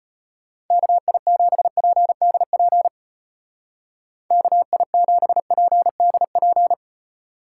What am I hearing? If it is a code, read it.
Morse code, heard as a keyed tone, about 30 wpm, 700 Hz. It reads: KI7PDP KI7PDP
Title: Speed Racing - USA 2x3 Call Signs 25wpm